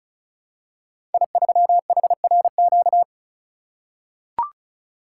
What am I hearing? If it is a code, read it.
Morse code, heard as a keyed tone, about 36 wpm, 700 Hz. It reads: I3HRQ
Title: Code Groups - 5 Letter:Numbers - 35wpm